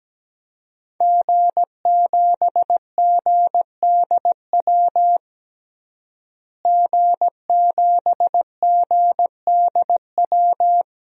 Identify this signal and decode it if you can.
Morse code, heard as a keyed tone, about 17 wpm, 700 Hz. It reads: G7GDW G7GDW
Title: DXCC-100 Call Signs - 5 Characters - Repeated 2x - 17wpm